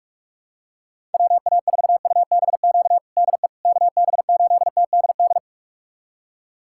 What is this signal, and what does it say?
Morse code, heard as a keyed tone, about 46 wpm, 700 Hz. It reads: WA4UBQ DE KB8TDD
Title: USA Callsign DE Callsign - 6 Characters - 45wpm